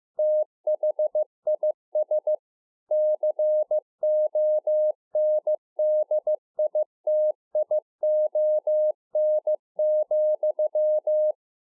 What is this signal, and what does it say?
Morse code, heard as a keyed tone, about 15 wpm, 610 Hz. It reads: THIS CONDITION,
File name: 15WPM